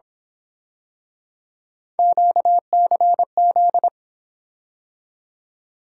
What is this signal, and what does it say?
Morse code, heard as a keyed tone, about 27 wpm, 700 Hz. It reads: QCZ
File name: Speed-Racing - Three Letter-Number Combinations - 22wpm